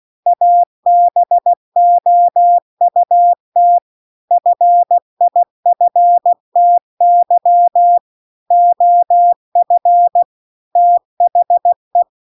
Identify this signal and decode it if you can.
Morse code, heard as a keyed tone, about 16 wpm, 700 Hz. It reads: ABOUT FIFTY OF THE
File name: CW-The-Big-Twoheaded-River-2-16WPM